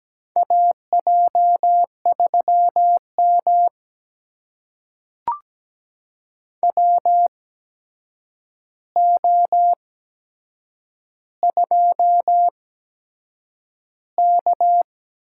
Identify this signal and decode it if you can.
Morse code, heard as a keyed tone, about 17 wpm, spaced as for 7 wpm, 700 Hz. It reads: AJ3M WO2K
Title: USA Call Signs - 4 Characters - 8x Character Spacing - 17wpm